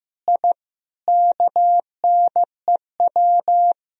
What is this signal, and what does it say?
Morse code, heard as a keyed tone, about 16 wpm, 700 Hz. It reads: I KNEW
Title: Sentences from Top 1000 Words - Review Part 3 - 15wpm